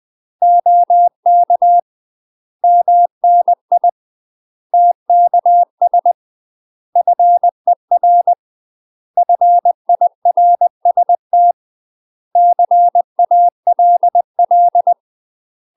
Morse code, heard as a keyed tone, about 20 wpm, 700 Hz. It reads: OK MNI TKS FER FIRST CALL
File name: UK MASTER QSOs TRAINING / 20WPM / UK MASTER QSOs 0022